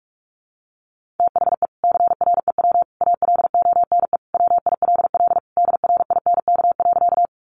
Morse code, heard as a keyed tone, about 44 wpm, 700 Hz. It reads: THE CREW ALONE WILL DRINK.
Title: Sentences from Top 1500 Words 45wpm